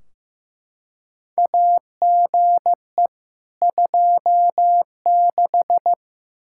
Morse code, heard as a keyed tone, about 15 wpm, 700 Hz. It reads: AGE 26
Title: Practice QSO - Exchange 3 - More Info - 15wpm